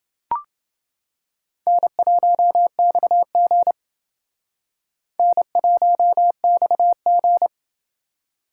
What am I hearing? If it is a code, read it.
Morse code, heard as a keyed tone, about 28 wpm, 700 Hz. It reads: N1XG N1XG